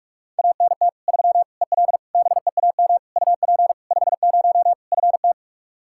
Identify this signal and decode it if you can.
Morse code, heard as a keyed tone, about 46 wpm, 700 Hz. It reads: ANT 3 EL BEAM UP 50 FT
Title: Practice QSO - Exchange 2 - Info - 45wpm